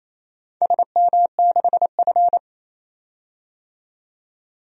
Morse code, heard as a keyed tone, about 28 wpm, 700 Hz. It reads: SM6F
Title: DXCC-100 Call Signs - 4 Characters - 28wpm